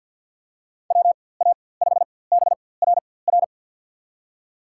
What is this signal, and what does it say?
Morse code, heard as a keyed tone, about 52 wpm, spaced as for 23 wpm, 700 Hz. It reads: WA5BRR
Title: USA Call Signs - 6 Characters - 4x Character Spacing - 50wpm